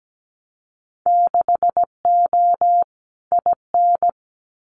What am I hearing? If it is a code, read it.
Morse code, heard as a keyed tone, about 17 wpm, 700 Hz. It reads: 6O IN
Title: Winter Field Day Exchanges Encore 17wpm